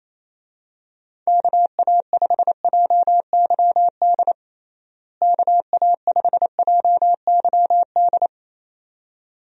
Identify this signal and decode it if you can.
Morse code, heard as a keyed tone, about 28 wpm, 700 Hz. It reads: KA5JYD KA5JYD
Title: USA Call Signs - 6 Characters - Repeated 2x - 28wpm